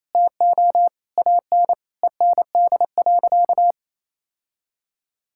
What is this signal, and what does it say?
Morse code, heard as a keyed tone, about 28 wpm, 700 Hz. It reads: TO AN END.
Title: Sentences from Top 1600 Words 28wpm